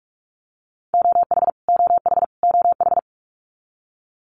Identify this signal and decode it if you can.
Morse code, heard as a keyed tone, about 44 wpm, 700 Hz. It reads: OH OH OH
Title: US States Repeated 3x 45wpm